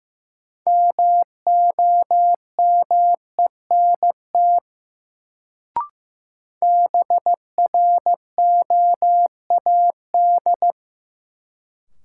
Morse code, heard as a keyed tone, about 15 wpm, 700 Hz. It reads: MOMENT BROAD
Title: Sentences from Top 1000 Words - New Word Drill - 15wpm